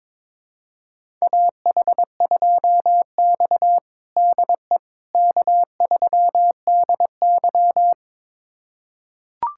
Morse code, heard as a keyed tone, about 22 wpm, 700 Hz. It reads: AH2X DE K3DY
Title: USA Callsign DE Callsign - 4 Characters - 22wpm